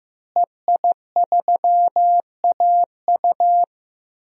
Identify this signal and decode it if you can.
Morse code, heard as a keyed tone, about 15 wpm, 700 Hz. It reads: EI3AU
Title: DXCC-100 Call Signs - 5 Characters - Repeated 2x - 15wpm